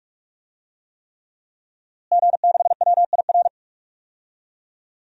Morse code, heard as a keyed tone, about 43 wpm, 700 Hz. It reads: G6WIR
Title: DXCC-100 Call Signs - 5 Characters - Repeated 3x - 45wpm